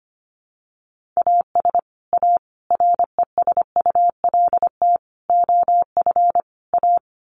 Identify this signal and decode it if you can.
Morse code, heard as a keyed tone, about 25 wpm, 700 Hz. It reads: AS A RESULT OF A
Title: Sets of 5 Words Encore 25wpm